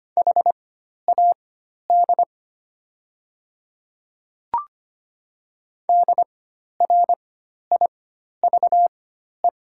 Morse code, heard as a keyed tone, about 25 wpm, spaced as for 12 wpm, 700 Hz. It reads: HAD DRIVE
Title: 4x Character Spacing - Top 500 Words - 25wpm